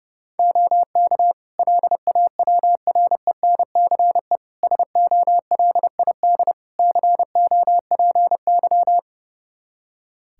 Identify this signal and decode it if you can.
Morse code, heard as a keyed tone, about 30 wpm, 700 Hz. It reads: OK LAWRENCE SOLID COPY
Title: Practice QSO - Exchange 3 - More Info - 30wpm